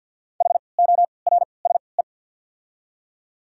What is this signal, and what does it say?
Morse code, heard as a keyed tone, about 51 wpm, spaced as for 28 wpm, 690 Hz. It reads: HORSE
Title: 3x Character Spacing - Top 300 Words - 50wpm